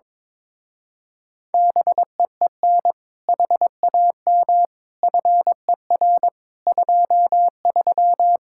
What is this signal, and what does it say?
Morse code, heard as a keyed tone, about 22 wpm, 700 Hz. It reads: BEEN HAM FER 23
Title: Speed-Racing Common QSO Phrases Encore - 22wpm